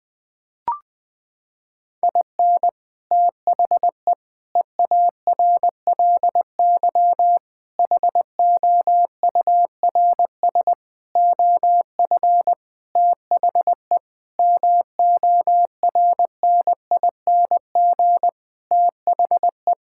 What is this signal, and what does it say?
Morse code, heard as a keyed tone, about 19 wpm, 700 Hz. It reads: IN THE EARLY HOURS OF THE MORNING THE
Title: Sentences with 7-Word Phrases - 20wpm